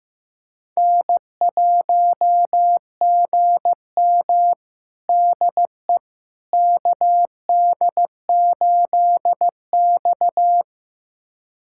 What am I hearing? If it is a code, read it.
Morse code, heard as a keyed tone, about 15 wpm, 700 Hz. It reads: N1GM DE KD8X
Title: USA Callsign DE Callsign - 4 Characters - 15wpm